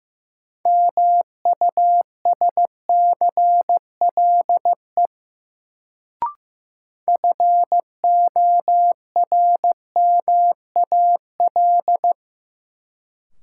Morse code, heard as a keyed tone, about 15 wpm, 700 Hz. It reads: MUSCLE FORMAL
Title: Sentences from Top 1300 Words - New Word Drill - 15wpm